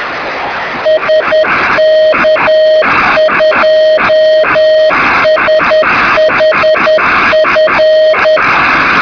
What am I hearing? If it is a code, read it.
Morse code, heard as a keyed tone, about 10 wpm, 630 Hz, with noise. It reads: SK2SHF
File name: sm7lcb / 2203z sk2shf-b 23cm